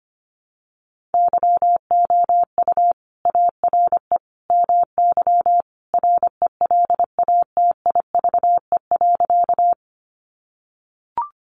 Morse code, heard as a keyed tone, about 25 wpm, 700 Hz. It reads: YOU ARE MY RELATIVE.
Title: Sentences from Top 1400 Words 25wpm